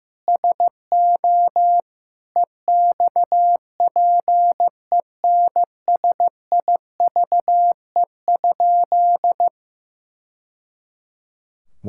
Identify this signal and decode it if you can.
Morse code, heard as a keyed tone, about 14 wpm, 700 Hz. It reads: SO EXPENSIVE?
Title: Sentences from Top 1000 Words - Review Part 6 - 15wpm